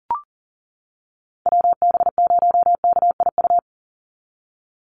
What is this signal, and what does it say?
Morse code, heard as a keyed tone, about 40 wpm, 700 Hz. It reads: WB0KIU
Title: NA SOTA Chaser Callsigns 40wpm